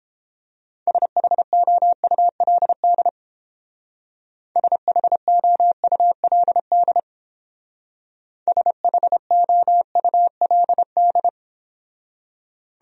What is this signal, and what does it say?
Morse code, heard as a keyed tone, about 30 wpm, 700 Hz. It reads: SHOULD SHOULD SHOULD